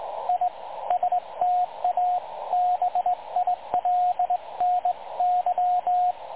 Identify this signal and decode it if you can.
Morse code, heard as a keyed tone, about 18 wpm, 720 Hz, with radio noise. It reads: I STABILNY
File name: dlazaawansowanych / nagraniazpasma / websdr recording 3565.0kHz